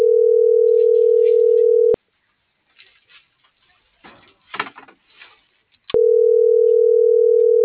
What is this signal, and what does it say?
Ringback tone (North America), 440 + 480 Hz.